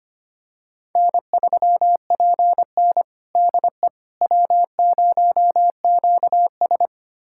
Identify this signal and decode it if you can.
Morse code, heard as a keyed tone, about 25 wpm, 700 Hz. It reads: N3PN DE W0QS
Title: USA Callsign DE Callsign - 4 Characters - 25wpm